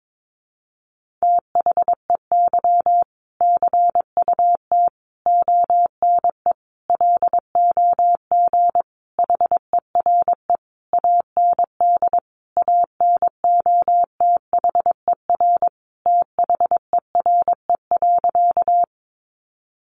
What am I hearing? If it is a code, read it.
Morse code, heard as a keyed tone, about 22 wpm, 700 Hz. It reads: THEY CUT ONE LOG HERE AND ANOTHER THERE.
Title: Sentences from Top 1400 Words 22wpm